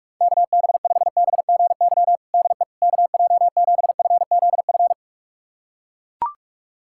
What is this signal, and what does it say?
Morse code, heard as a keyed tone, about 42 wpm, 700 Hz. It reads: KB5BGY DE KJ7FZF
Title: USA Callsign DE Callsign 45wpm